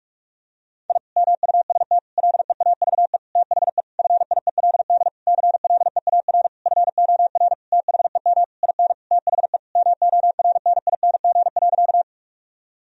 Morse code, heard as a keyed tone, about 46 wpm, 700 Hz. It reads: I MUST LEAVE THE FIELD CLEAR FOR THEM IN THE MORNING.
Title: Sentences from Top 1000 Words - Review Part 2 - 45wpm